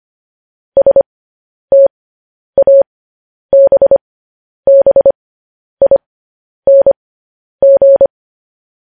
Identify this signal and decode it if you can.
Morse code, heard as a keyed tone, about 25 wpm, spaced as for 10 wpm, 550 Hz. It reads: STABBING
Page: news headlines in morse code at 10 wpm